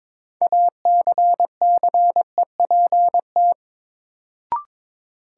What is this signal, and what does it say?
Morse code, heard as a keyed tone, about 22 wpm, 700 Hz. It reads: ACCEPT
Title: Sentences from Top 1300 Words - New Word Drill - 22wpm